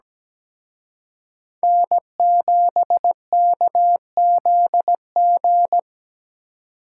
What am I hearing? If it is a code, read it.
Morse code, heard as a keyed tone, about 20 wpm, 700 Hz. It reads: N7KZG
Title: Speed Racing - USA Call Signs 17wpm